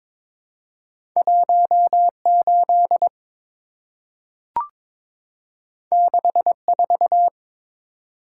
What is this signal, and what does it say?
Morse code, heard as a keyed tone, about 22 wpm, 700 Hz. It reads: 18 64
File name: practice / 22wpm / Two Letter-Number Combinations - 22wpm